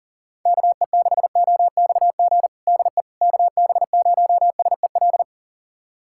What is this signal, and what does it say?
Morse code, heard as a keyed tone, about 40 wpm, 700 Hz. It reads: KE6OXG DE KB0SEL
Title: USA Callsign DE Callsign - 6 Characters - 40wpm